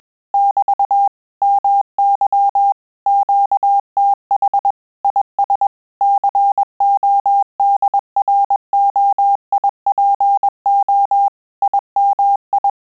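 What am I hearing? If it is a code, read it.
Morse code, heard as a keyed tone, about 21 wpm, 800 Hz. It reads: <BT> MY QTH IS CODROIPO IMI
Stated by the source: Morse QSO practice at 20 words per minute, in Italian and English.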